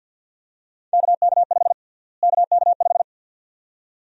Morse code, heard as a keyed tone, about 50 wpm, 700 Hz. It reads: KK5 KK5
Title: USA Call Sign Prefixes - Repeated 2x - 50wpm